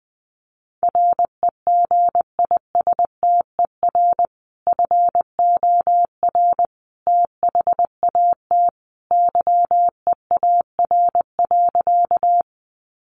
Morse code, heard as a keyed tone, about 20 wpm, 700 Hz. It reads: REGISTER FOR THAT YEAR.
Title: Sentences from Top 1200 Words 20wpm